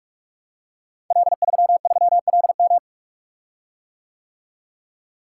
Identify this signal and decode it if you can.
Morse code, heard as a keyed tone, about 44 wpm, 700 Hz. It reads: L33LM
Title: Code Groups - 5 Letter:Numbers - 45wpm